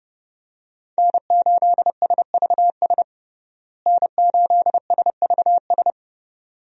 Morse code, heard as a keyed tone, about 32 wpm, 700 Hz. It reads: N8SVS N8SVS
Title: USA Call Signs - 5 Characters - Repeated 2x - 30wpm